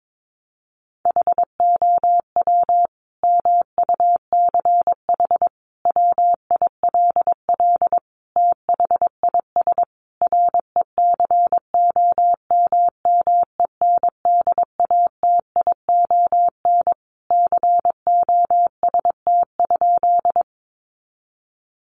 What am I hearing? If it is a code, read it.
Morse code, heard as a keyed tone, about 22 wpm, 700 Hz. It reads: HOW MUCH WILL THIS RECOMMENDATION COST?
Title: Sentences from Top 1400 Words 22wpm